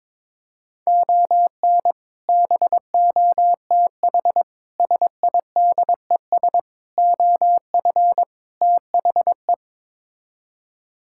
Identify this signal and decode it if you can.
Morse code, heard as a keyed tone, about 21 wpm, 700 Hz. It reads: ON BOTH SIDES OF THE
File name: practice / 22wpm / Build A Phrase - 5-Word Phrases - 22wpm